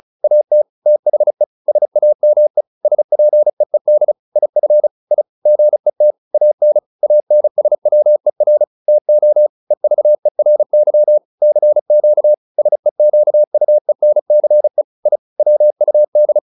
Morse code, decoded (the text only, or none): AT THE SAME SPEED IF I GET AN ANSWER TO EVERY CQ SEQUENCE I WUD